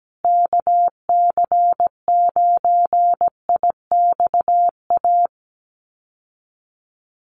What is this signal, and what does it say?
Morse code, heard as a keyed tone, about 17 wpm, 700 Hz. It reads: KC9IXA
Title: USA Call Signs Repeated 3x 17wpm